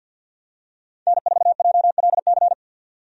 Morse code, heard as a keyed tone, about 46 wpm, 700 Hz. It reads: N4JLC
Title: USA Call Signs - 5 Characters - 50wpm